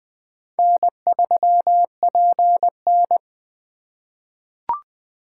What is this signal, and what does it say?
Morse code, heard as a keyed tone, about 20 wpm, 700 Hz. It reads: N3PN
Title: USA Call Signs - 4 Characters - 3x Character Spacing - 20wpm